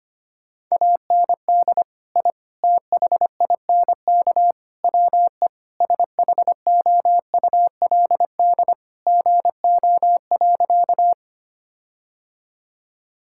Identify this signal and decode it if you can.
Morse code, heard as a keyed tone, about 25 wpm, 700 Hz. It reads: AND I THINK WE SHOULD GO.
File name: practice / 25wpm / Sentences with 4-Word Phrases - 25wpm